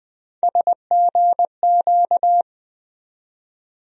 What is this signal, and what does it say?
Morse code, heard as a keyed tone, about 21 wpm, 700 Hz. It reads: SGQ
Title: USA Call Sign Postfixes - 3 Letters 20wpm